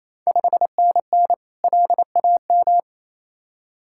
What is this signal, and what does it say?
Morse code, heard as a keyed tone, about 28 wpm, 690 Hz. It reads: 5NN LAM
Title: Ontario QSO Party Exchange 28wpm